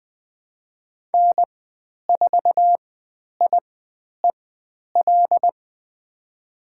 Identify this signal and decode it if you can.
Morse code, heard as a keyed tone, about 20 wpm, spaced as for 10 wpm, 700 Hz. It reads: N4IEL
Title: 095 Lesson 16g - Letter c - Call Signs 20wpm